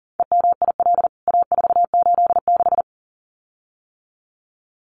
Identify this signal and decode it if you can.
Morse code, heard as a keyed tone, about 40 wpm, 700 Hz. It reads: EMIL A486
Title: CWT Exchange 40wpm